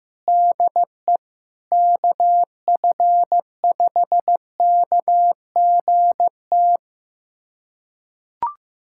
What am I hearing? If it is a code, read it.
Morse code, heard as a keyed tone, about 15 wpm, 700 Hz. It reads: DE KF5KGT
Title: USA Callsign DE Callsign - 6 Characters - 15wpm